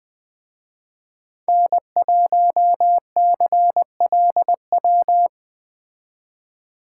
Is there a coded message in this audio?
Yes: Morse code from 1.5 to 5.3 s, heard as a keyed tone, about 20 wpm, 700 Hz. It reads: N1CLW